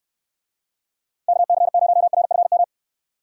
Morse code, heard as a keyed tone, about 69 wpm, 700 Hz. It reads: KC0RFM